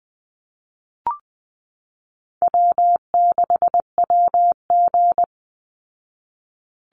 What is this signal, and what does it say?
Morse code, heard as a keyed tone, about 20 wpm, 700 Hz. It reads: W6WG
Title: Call Signs Repeated 2x - 20wpm